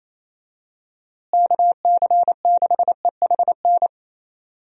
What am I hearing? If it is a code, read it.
Morse code, heard as a keyed tone, about 28 wpm, 700 Hz. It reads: KC6EHN
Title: USA Call Signs - 6 Characters - Repeated 3x - 28wpm